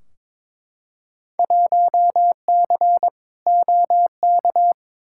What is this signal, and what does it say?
Morse code, heard as a keyed tone, about 22 wpm, 700 Hz. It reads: 1C OK
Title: Field Day Exchanges Encore - 22wpm